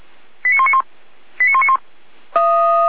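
Five-tone selective call (ZVEI), heard as quick runs of five tones, tones of about 70 ms. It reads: 89191 89191
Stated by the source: ZVEI five-tone call received on an SDR.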